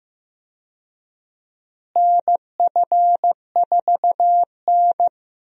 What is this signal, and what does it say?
Morse code, heard as a keyed tone, about 15 wpm, 700 Hz. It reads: NF4N
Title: USA Call Signs - 4 Characters - Repeated 2x - 15wpm